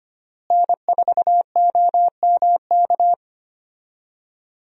Morse code, heard as a keyed tone, about 25 wpm, 700 Hz. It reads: N4OMK
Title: USA Call Signs - 5 Characters - Repeated 3x - 25wpm